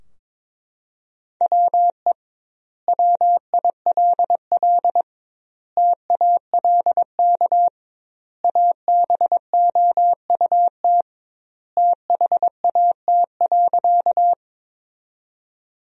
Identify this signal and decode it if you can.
Morse code, heard as a keyed tone, about 22 wpm, 700 Hz. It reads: WE WILL TALK ABOUT THAT.
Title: 2x Word-Spacing - Sentences from Top 400 Words - 22wpm